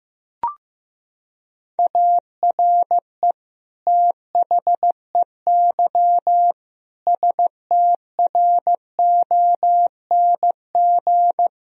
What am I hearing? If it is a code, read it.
Morse code, heard as a keyed tone, about 14 wpm, 700 Hz. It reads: ARE THEY STRONG
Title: Sentences from Top 1000 Words - Review Part 3 - 15wpm